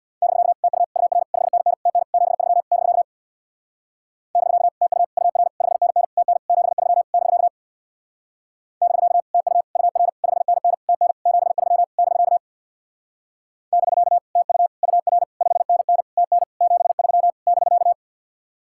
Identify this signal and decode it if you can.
Morse code, heard as a keyed tone, about 66 wpm, 700 Hz. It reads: <BK> TU UR 5NN TN 73 <BK> <BK> TU UR 5NN TN 73 <BK> <BK> TU UR 5NN TN 73 <BK> <BK> TU UR 5NN TN 73 <BK>